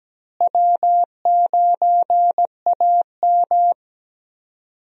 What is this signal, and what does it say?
Morse code, heard as a keyed tone, about 17 wpm, 700 Hz. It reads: W9AM
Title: USA Call Signs - 4 Characters - Repeated 3x - 17wpm